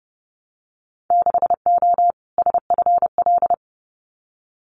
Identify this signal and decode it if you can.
Morse code, heard as a keyed tone, about 30 wpm, 700 Hz. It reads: 6O SFL
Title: Winter Field Day Exchanges Encore 30wpm